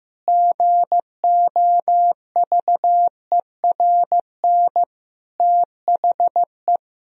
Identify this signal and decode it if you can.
Morse code, heard as a keyed tone, about 15 wpm, 700 Hz. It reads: GOVERN THE
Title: Sentences from Top 1000 Words - Review Part 2 - 15wpm